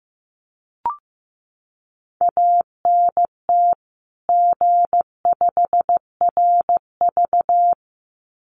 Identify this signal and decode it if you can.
Morse code, heard as a keyed tone, about 15 wpm, 700 Hz. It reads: ANT G5RV
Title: Common QSO Phrases 15wpm